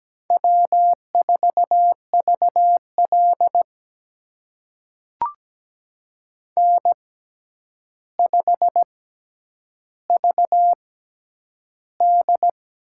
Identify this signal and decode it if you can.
Morse code, heard as a keyed tone, about 17 wpm, spaced as for 9 wpm, 700 Hz. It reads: W4VL N5VD
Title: USA Call Signs - 4 Characters - 6x Character Spacing - 17wpm